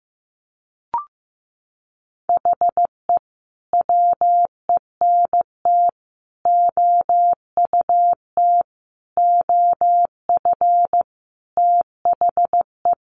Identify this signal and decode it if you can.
Morse code, heard as a keyed tone, about 15 wpm, 700 Hz. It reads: HE WENT OUT OF THE
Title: Sentences from Top 1100 Words 15wpm